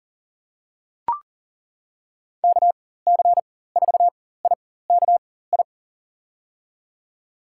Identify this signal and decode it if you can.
Morse code, heard as a keyed tone, about 40 wpm, spaced as for 19 wpm, 700 Hz. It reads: KC4IKI
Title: USA Call Signs - 6 Characters - 4x Character Spacing - 40wpm